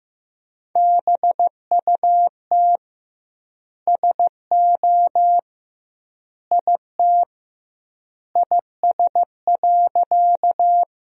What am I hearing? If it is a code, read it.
Morse code, heard as a keyed tone, about 16 wpm, 700 Hz. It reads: BUT SO IT IS.
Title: 2x Word-Spacing - Sentences from Top 100 Words - 15wpm